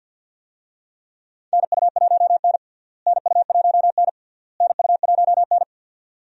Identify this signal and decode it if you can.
Morse code, heard as a keyed tone, about 46 wpm, 700 Hz. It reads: NU1N NU1N NU1N
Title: USA Call Signs - 4 Characters - Repeated 3x - 50wpm